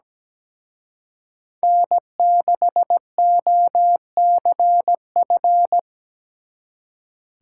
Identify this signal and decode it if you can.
Morse code, heard as a keyed tone, about 20 wpm, 700 Hz. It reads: N6OCF
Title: Speed Racing - USA 1x3 Call Signs 17wpm